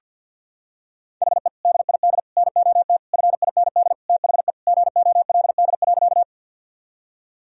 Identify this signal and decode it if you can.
Morse code, heard as a keyed tone, about 50 wpm, 700 Hz. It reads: HE DID NOT FIND THE GOLD.
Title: Sentences from Top 1000 Words - Review Part 3 - 50wpm